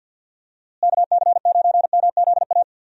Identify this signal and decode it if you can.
Morse code, heard as a keyed tone, about 50 wpm, 700 Hz. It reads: KC9MZA
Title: USA Call Signs - 6 Characters - Repeated 3x - 50wpm